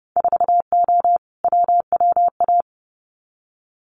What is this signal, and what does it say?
Morse code, heard as a keyed tone, about 30 wpm, 700 Hz. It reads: 4O WWA
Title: Winter Field Day Exchanges 30wpm